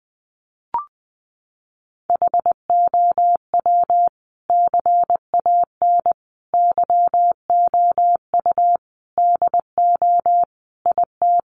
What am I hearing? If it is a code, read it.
Morse code, heard as a keyed tone, about 20 wpm, 700 Hz. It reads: HOW CAN YOU DO IT
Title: Sentences from Top 1200 Words 20wpm